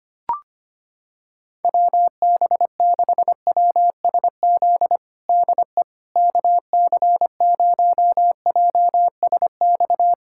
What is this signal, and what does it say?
Morse code, heard as a keyed tone, about 25 wpm, 700 Hz. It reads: WB6WSZ DE KC0JSX
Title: USA Callsign DE Callsign 25wpm